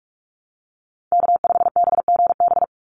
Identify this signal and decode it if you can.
Morse code, heard as a keyed tone, about 44 wpm, 700 Hz. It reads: K5BGB
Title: Call Signs - 45wpm